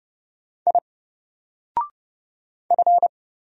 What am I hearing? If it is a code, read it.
Morse code, heard as a keyed tone, about 30 wpm, 700 Hz. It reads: I F
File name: Single-Letter-Number-30wpm